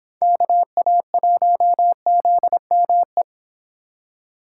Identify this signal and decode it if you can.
Morse code, heard as a keyed tone, about 24 wpm, 700 Hz. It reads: KA1ZME